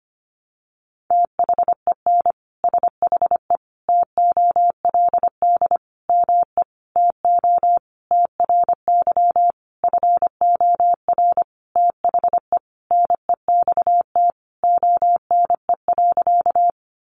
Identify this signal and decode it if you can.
Morse code, heard as a keyed tone, about 25 wpm, 700 Hz. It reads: THEN SHE TOLD ME TO TRY FOR THE NEXT ONE.
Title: Sentences from Top 400 Words 25wpm